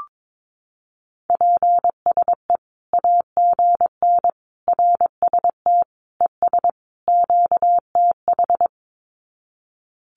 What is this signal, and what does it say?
Morse code, heard as a keyed tone, about 22 wpm, 700 Hz. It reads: PSE AGN RST ES QTH
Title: Common QSO Phrases 22wpm